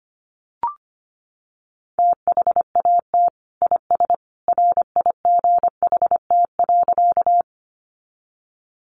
Morse code, heard as a keyed tone, about 25 wpm, 700 Hz. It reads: THAT IS RIGHT.
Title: Sentences from Top 200 Words 25wpm